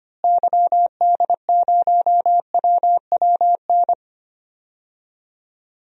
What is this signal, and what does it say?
Morse code, heard as a keyed tone, about 25 wpm, 700 Hz. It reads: YD0WWN
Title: DXCC-100 Call Signs - 6 Characters - Repeated 3x - 25wpm